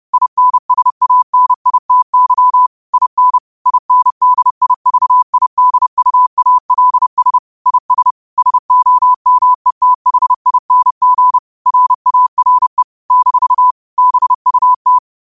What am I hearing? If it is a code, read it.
Morse code, heard as a keyed tone, about 30 wpm, 1000 Hz. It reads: INSANITY IN INDIVIDUALS IS SOMETHING RARE - BUT
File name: qotdfiles / 30wpm